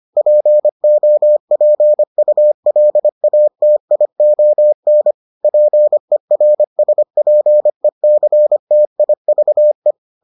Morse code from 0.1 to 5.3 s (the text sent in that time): POPULATION